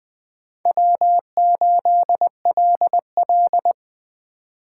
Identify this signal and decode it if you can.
Morse code, heard as a keyed tone, about 21 wpm, 700 Hz. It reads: W8LL